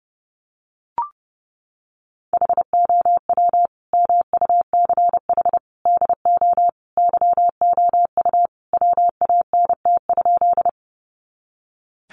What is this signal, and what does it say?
Morse code, heard as a keyed tone, about 30 wpm, 700 Hz. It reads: HOW MUCH DO YOU WANT?
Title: Sentences from Top 200 Words 30wpm